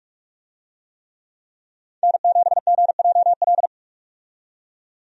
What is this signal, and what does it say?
Morse code, heard as a keyed tone, about 43 wpm, 700 Hz. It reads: N7GJL
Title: USA Call Signs - 5 Characters - 45wpm